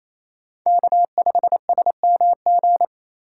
Morse code, heard as a keyed tone, about 28 wpm, 700 Hz. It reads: K5SMG
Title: USA Call Signs - 5 Characters - 2x Character Spacing - 28wpm